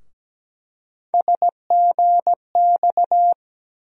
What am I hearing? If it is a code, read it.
Morse code, heard as a keyed tone, about 17 wpm, 700 Hz. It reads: SGX
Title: USA Call Sign Postfixes - 3 Letters 17wpm